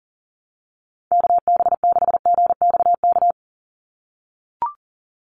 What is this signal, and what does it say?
Morse code, heard as a keyed tone, about 40 wpm, 700 Hz. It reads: KB6GXK
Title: USA Call Signs 40wpm